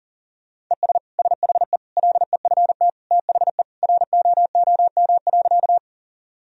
Morse code, heard as a keyed tone, about 40 wpm, 700 Hz. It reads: ES SHE LEFT THE ROOM.
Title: Sentences from Top 300 Words and Abbreviations 40wpm